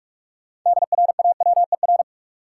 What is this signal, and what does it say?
Morse code, heard as a keyed tone, about 46 wpm, 700 Hz. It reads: DRAWER
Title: Six Letter Words - 45wpm